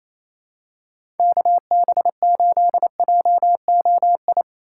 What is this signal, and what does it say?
Morse code, heard as a keyed tone, about 28 wpm, 700 Hz. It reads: KB8JOI
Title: USA Call Signs - 6 Characters - 3x Character Spacing - 28wpm